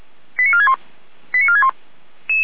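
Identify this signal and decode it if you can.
Five-tone selective call (ZVEI), heard as quick runs of five tones, tones of about 70 ms. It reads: 89461 89461
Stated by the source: Five-tone ZVEI selective call received on an SDR.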